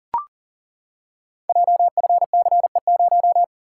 Morse code, heard as a keyed tone, about 42 wpm, 690 Hz. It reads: JFCE0
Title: Code Groups - 5 Letter:Numbers - 40wpm